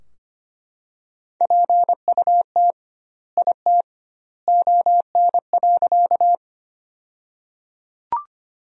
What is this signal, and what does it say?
Morse code, heard as a keyed tone, about 25 wpm, 700 Hz. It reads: PUT IT ON.
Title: 2x Word-Spacing - Sentences from Top 200 Words - 25wpm